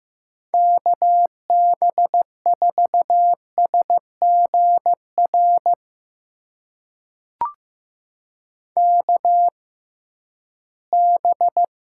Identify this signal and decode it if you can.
Morse code, heard as a keyed tone, about 15 wpm, spaced as for 11 wpm, 700 Hz. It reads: KB4SGR KB
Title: USA Call Signs - 6 Characters - 6x Character Spacing - 15wpm